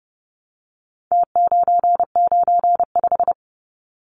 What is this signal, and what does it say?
Morse code, heard as a keyed tone, about 30 wpm, 700 Hz. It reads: T995
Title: Numbers - Contest Style - 4-Digit 30wpm